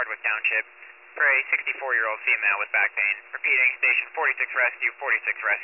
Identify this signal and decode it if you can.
Emergency Alert System (SAME) tones, heard as harsh buzzing bursts, with speech. It is heard from 1.1 to 5.2 s.